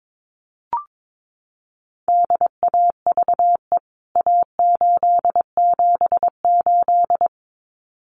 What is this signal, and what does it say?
Morse code, heard as a keyed tone, about 22 wpm, 700 Hz. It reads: DAVE A878
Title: CWT Exchange 22wpm